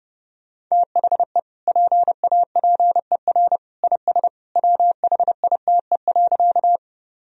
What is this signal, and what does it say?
Morse code, heard as a keyed tone, about 30 wpm, 700 Hz. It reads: THE PAPER IS WHITE.
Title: Sentences from Top 1000 Words - Review Part 1 - 30wpm